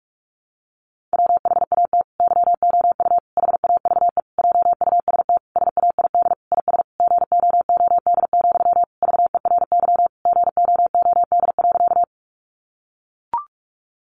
Morse code, heard as a keyed tone, about 44 wpm, 700 Hz. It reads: WHAT YOU HAVE JUST SAID IS GOOD, VERY GOOD.
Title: Sentences from Top 200 Words 45wpm